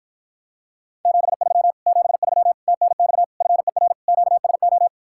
Morse code, heard as a keyed tone, about 54 wpm, 700 Hz. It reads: 73 73 TNX FER QSO